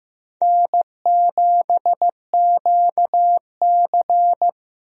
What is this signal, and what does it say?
Morse code, heard as a keyed tone, about 15 wpm, 700 Hz. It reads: N7QC
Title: USA Call Signs - 4 Characters - Repeated 2x - 15wpm